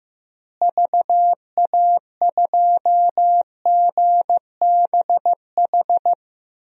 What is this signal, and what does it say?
Morse code, heard as a keyed tone, about 16 wpm, 700 Hz. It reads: VA2GBH
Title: DXCC-100 Call Signs - 6 Characters - Repeated 3x - 15wpm